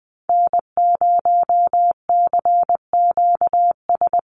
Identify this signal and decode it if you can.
Morse code, heard as a keyed tone, about 20 wpm, 700 Hz. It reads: N0CQS
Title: USA Call Signs 20wpm